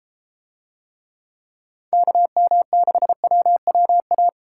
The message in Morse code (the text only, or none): KM6WWA